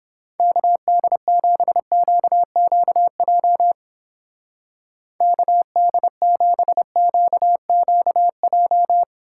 Morse code, heard as a keyed tone, about 31 wpm, 700 Hz. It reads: KD7QQJ KD7QQJ